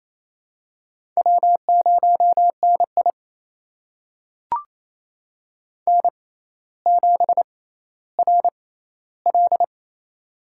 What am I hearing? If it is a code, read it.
Morse code, heard as a keyed tone, about 27 wpm, spaced as for 14 wpm, 700 Hz. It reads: W0NI N7RL
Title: USA Call Signs - 4 Characters - 6x Character Spacing - 28wpm